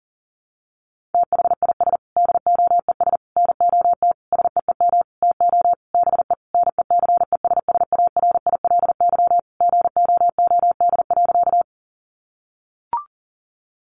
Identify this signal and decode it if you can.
Morse code, heard as a keyed tone, about 40 wpm, 700 Hz. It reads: THIS DOES NOT SEEM TO BE NECESSARILY GOOD.
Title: Sentences from Top 1400 Words 40wpm